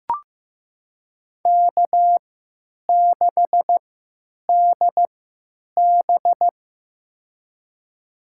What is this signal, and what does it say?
Morse code, heard as a keyed tone, about 16 wpm, spaced as for 9 wpm, 700 Hz. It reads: K6DB
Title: USA Call Signs - 4 Characters - 3x Character Spacing - 15wpm